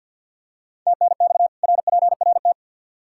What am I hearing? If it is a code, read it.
Morse code, heard as a keyed tone, about 50 wpm, 700 Hz. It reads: TNX RPRT
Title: K1USN SST QSO Practice 50wpm